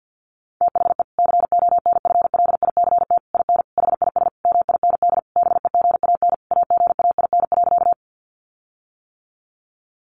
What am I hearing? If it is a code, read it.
Morse code, heard as a keyed tone, about 50 wpm, 700 Hz. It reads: THE CONFLICT IN HIS MIND BEGAN AGAIN.
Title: Sentences from Top 1300 Words 50wpm